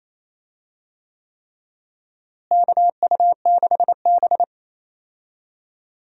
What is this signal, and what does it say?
Morse code, heard as a keyed tone, about 28 wpm, 700 Hz. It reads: KU6B
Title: USA Call Signs - 4 Characters - Repeated 3x - 28wpm